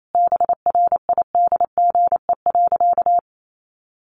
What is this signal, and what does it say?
Morse code, heard as a keyed tone, about 28 wpm, 700 Hz. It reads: BRIDGE.
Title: Sentences from Top 1200 Words 28wpm